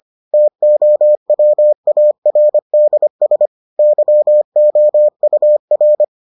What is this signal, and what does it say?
Morse code, heard as a keyed tone, about 25 wpm, 600 Hz. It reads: TOWARDS YOUR